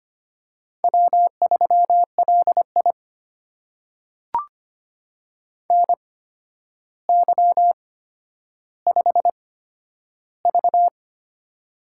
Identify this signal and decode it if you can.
Morse code, heard as a keyed tone, about 25 wpm, spaced as for 11 wpm, 700 Hz. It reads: W3LI NY5V
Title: USA Call Signs - 4 Characters - 8x Character Spacing - 25wpm